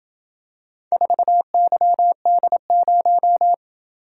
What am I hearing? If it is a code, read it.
Morse code, heard as a keyed tone, about 26 wpm, 700 Hz. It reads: 4YD0